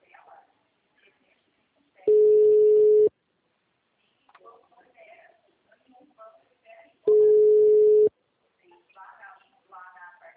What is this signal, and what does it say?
Ringback tone (Europe, 425 Hz), with speech.